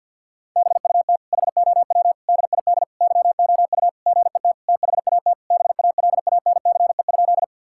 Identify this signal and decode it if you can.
Morse code, heard as a keyed tone, about 52 wpm, 700 Hz. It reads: BUT HOW DID YOU GET THAT BALANCE?
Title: Sentences from Top 1000 Words - Review Part 6 - 50wpm